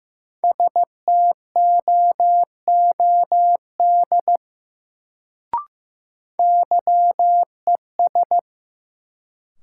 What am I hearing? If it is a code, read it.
Morse code, heard as a keyed tone, about 15 wpm, 700 Hz. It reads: STOOD YES
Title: Sentences from Top 500 Words - New Word Drill - 15wpm